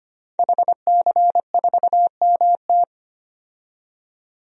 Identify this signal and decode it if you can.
Morse code, heard as a keyed tone, about 25 wpm, 700 Hz. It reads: HC4MT
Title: DXCC-100 Call Signs - 5 Characters - 25wpm